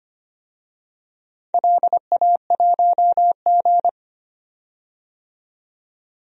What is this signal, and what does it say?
Morse code, heard as a keyed tone, about 25 wpm, 700 Hz. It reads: LA1G
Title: DXCC-100 Call Signs - 4 Characters - 25wpm